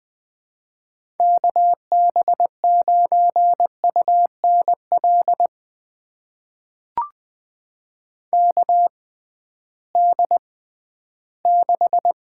Morse code, heard as a keyed tone, about 20 wpm, spaced as for 13 wpm, 700 Hz. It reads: KB9UNL KD6
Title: USA Call Signs - 6 Characters - 6x Character Spacing - 20wpm